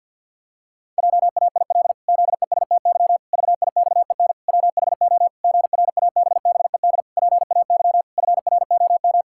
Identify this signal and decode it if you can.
Morse code, heard as a keyed tone, about 51 wpm, 700 Hz. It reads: JAIL ZESTY VIXEN WHO GRABBED PAY FROM
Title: Pangram Challenge v2 - 50wpm